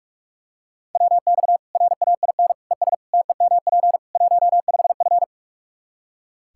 Morse code, heard as a keyed tone, about 45 wpm, 700 Hz. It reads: WX RAIN ES TEMP 15F
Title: Practice QSO - Exchange 2 - Info - 45wpm